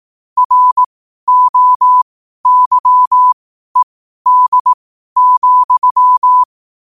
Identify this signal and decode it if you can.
Morse code, heard as a keyed tone, about 18 wpm, spaced as for 12 wpm, 1000 Hz. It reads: ROYED,
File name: qotdfiles / 12wpm